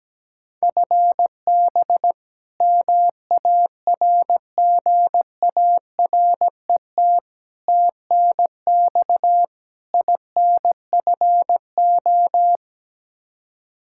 Morse code, heard as a keyed tone, about 17 wpm, 700 Hz. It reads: FB MARGARET TNX INFO
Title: Practice QSO - Exchange 3 - More Info - 17wpm